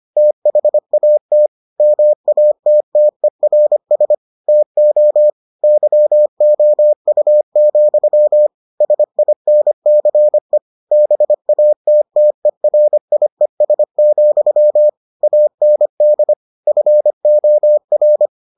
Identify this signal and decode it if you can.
Morse code, heard as a keyed tone, about 25 wpm, 600 Hz. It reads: THAT MATTERS TO YOU, SINCE BATTERIES, AND FOR